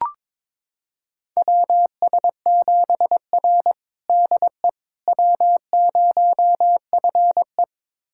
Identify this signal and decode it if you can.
Morse code, heard as a keyed tone, about 23 wpm, 700 Hz. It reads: WS7R DE W0FE
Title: USA Callsign DE Callsign - 4 Characters - 22wpm